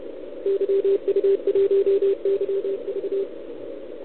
Morse code, heard as a keyed tone, about 31 wpm, 440 Hz, with radio noise. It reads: YU1YV
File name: ARCHIV / OKDX2020 / OL4Wa